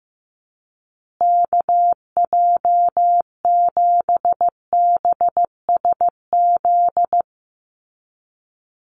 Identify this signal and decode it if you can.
Morse code, heard as a keyed tone, about 15 wpm, 700 Hz. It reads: KJ7BSZ
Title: USA Call Signs Repeated 2x 15wpm